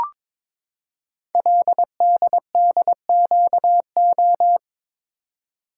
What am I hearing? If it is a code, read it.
Morse code, heard as a keyed tone, about 22 wpm, 700 Hz. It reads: LDDQO
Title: Code Groups - 5 Letters - 22wpm